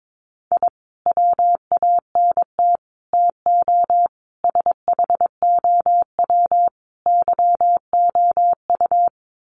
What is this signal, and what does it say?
Morse code, heard as a keyed tone, about 22 wpm, 700 Hz. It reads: I WANT TO SHOW YOU
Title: Sets of 5 Words 22wpm